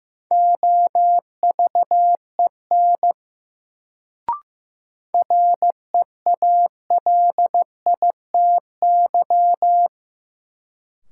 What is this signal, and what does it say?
Morse code, heard as a keyed tone, about 15 wpm, 700 Hz. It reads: OVEN REALITY
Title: Sentences from Top 1200 Words - New Word Drill Alt - 15wpm